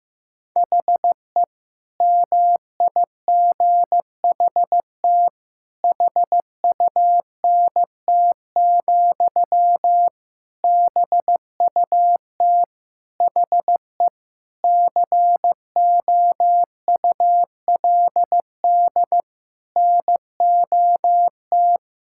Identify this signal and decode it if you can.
Morse code, heard as a keyed tone, about 15 wpm, 700 Hz. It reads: HE MIGHT HUNT, BUT HE COULD NOT
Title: Sentences from Top 1600 Words 15wpm